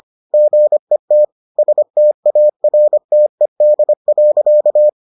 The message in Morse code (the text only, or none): GET STARTED.